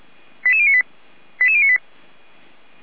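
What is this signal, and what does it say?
Five-tone selective call (ZVEI), heard as quick runs of five tones, tones of about 70 ms. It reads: 88098 88098
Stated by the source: ZVEI five-tone selective call received on an SDR.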